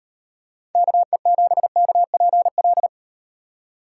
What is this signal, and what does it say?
Morse code, heard as a keyed tone, about 38 wpm, 700 Hz. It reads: KE7KPL